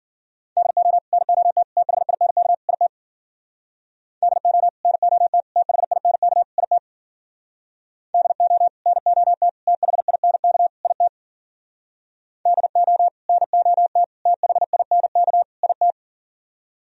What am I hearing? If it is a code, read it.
Morse code, heard as a keyed tone, about 52 wpm, 700 Hz. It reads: DO NOT THINK IT DO NOT THINK IT DO NOT THINK IT DO NOT THINK IT